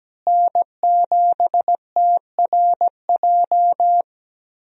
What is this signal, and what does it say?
Morse code, heard as a keyed tone, about 17 wpm, 700 Hz. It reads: N7TRJ
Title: USA Call Signs - 5 Characters - Repeated 3x - 17wpm